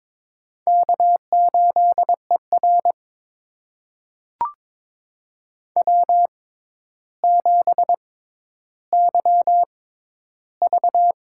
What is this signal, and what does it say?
Morse code, heard as a keyed tone, about 22 wpm, spaced as for 11 wpm, 700 Hz. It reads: K8ER W7YV
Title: USA Call Signs - 4 Characters - 6x Character Spacing - 22wpm